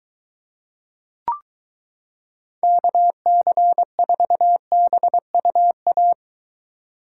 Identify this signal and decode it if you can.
Morse code, heard as a keyed tone, about 23 wpm, 700 Hz. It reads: KC4BUA